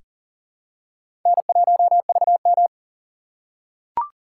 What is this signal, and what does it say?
Morse code, heard as a keyed tone, about 40 wpm, 700 Hz. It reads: N1VM
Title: USA Call Signs - 4 Characters - 3x Character Spacing - 40wpm